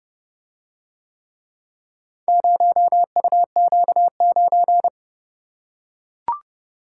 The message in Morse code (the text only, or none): 0UQ9